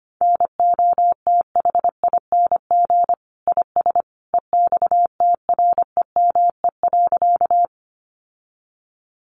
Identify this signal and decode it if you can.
Morse code, heard as a keyed tone, about 25 wpm, 700 Hz. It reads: NOTHING IS EXTREME.
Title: Sentences from Top 1300 Words 25wpm